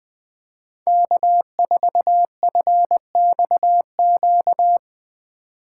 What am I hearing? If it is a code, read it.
Morse code, heard as a keyed tone, about 20 wpm, 700 Hz. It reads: K4FXQ